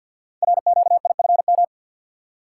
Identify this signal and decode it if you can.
Morse code, heard as a keyed tone, about 48 wpm, 700 Hz. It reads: RQIFM
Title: Code Groups - 5 Letters - 50wpm